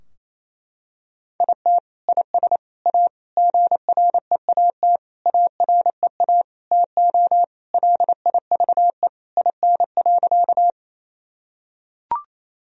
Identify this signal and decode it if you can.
Morse code, heard as a keyed tone, about 28 wpm, 700 Hz. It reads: IT IS A GREAT AREA TO LIVE IN.
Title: Sentences from Top 1000 Words - Review Part 2 - 28wpm